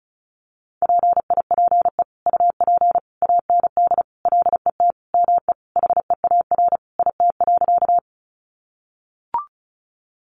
Morse code, heard as a keyed tone, about 34 wpm, 700 Hz. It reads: PIPE UP AND LET ME HEAR IT.
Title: Sentences from Top 1500 Words 35wpm